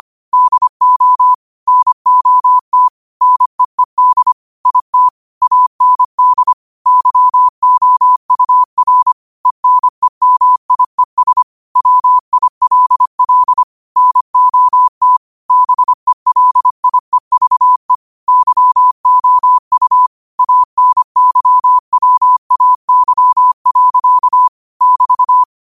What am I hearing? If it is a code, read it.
Morse code, heard as a keyed tone, about 25 wpm, 1000 Hz. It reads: DO NOT NEED IT AND YOUR ENEMIES WILL NOT BELIEVE YOU ANYWAY. <BT>